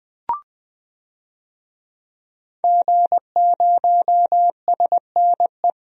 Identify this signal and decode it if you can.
Morse code, heard as a keyed tone, about 20 wpm, 700 Hz. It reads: G0SNE
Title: DXCC-100 Call Signs - 5 Characters - 20wpm